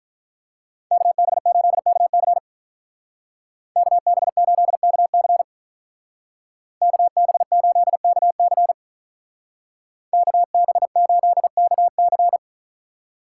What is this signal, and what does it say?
Morse code, heard as a keyed tone, about 45 wpm, 700 Hz. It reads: KB8KC KB8KC KB8KC KB8KC